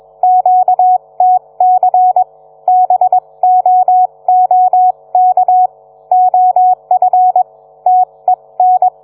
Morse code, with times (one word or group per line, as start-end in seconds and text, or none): 0.2-2.2 QTC
2.7-5.7 BOOK
6.1-7.4 OF
7.9-8.9 TEN